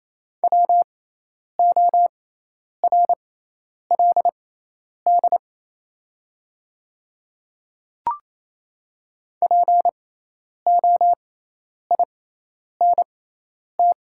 Morse code, heard as a keyed tone, about 28 wpm, spaced as for 10 wpm, 700 Hz. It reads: WORLD POINT
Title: 6x Character Spacing - Top 200 Words - 28wpm